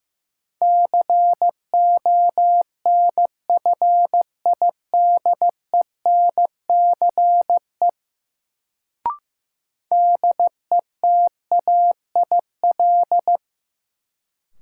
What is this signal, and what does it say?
Morse code, heard as a keyed tone, about 15 wpm, 700 Hz. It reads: CONFIDENCE DETAIL
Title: Sentences from Top 1500 Words - New Word Drill Alt - 15wpm